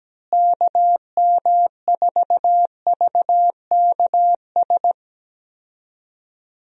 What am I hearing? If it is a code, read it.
Morse code, heard as a keyed tone, about 17 wpm, 700 Hz. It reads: KM4VKS
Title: USA Call Signs - 6 Characters - Repeated 2x - 17wpm